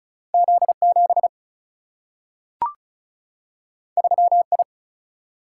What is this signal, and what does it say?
Morse code, heard as a keyed tone, about 36 wpm, 700 Hz. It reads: Z7 3I
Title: Single-Letter Single-Number 35wpm